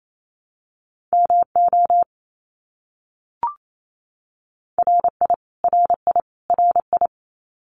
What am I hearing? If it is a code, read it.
Morse code, heard as a keyed tone, about 28 wpm, 700 Hz. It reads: MO RI RI RI
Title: US States Repeated 3x 28wpm